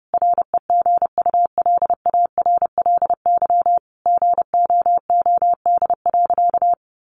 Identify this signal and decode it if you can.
Morse code, heard as a keyed tone, about 30 wpm, 700 Hz. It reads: REGULARLY GOOD.
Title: Sentences from Top 1400 Words 30wpm